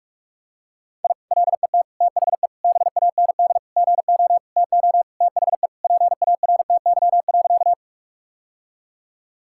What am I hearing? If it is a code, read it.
Morse code, heard as a keyed tone, about 46 wpm, 700 Hz. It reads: I LET THE BAND GO TO THE PARTY.
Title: Sentences from Top 1600 Words 45wpm